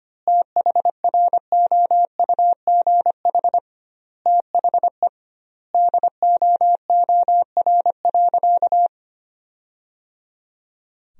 Morse code, heard as a keyed tone, about 25 wpm, 700 Hz. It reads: THROUGH THE DOOR.
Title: 2x Word-Spacing - Sentences from Top 400 Words - 25wpm